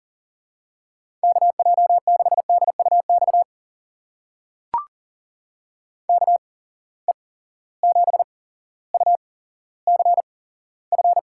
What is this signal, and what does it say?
Morse code, heard as a keyed tone, about 40 wpm, spaced as for 17 wpm, 700 Hz. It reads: KJ6DUX KE7UCF
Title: USA Call Signs - 6 Characters - 8x Character Spacing - 40wpm